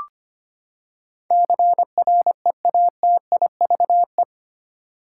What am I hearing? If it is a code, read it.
Morse code, heard as a keyed tone, about 25 wpm, 700 Hz. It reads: CREATIVE
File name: Eight Letter Words - 25wpm